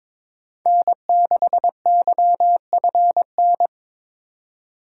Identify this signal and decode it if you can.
Morse code, heard as a keyed tone, about 26 wpm, 700 Hz. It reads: N6YFN
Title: Speed Racing - USA 1x3 Call Signs 22wpm